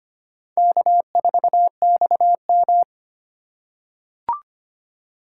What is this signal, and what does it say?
Morse code, heard as a keyed tone, about 25 wpm, 700 Hz. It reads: K4XM
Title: USA Call Signs - 4 Characters - 2x Character Spacing - 25wpm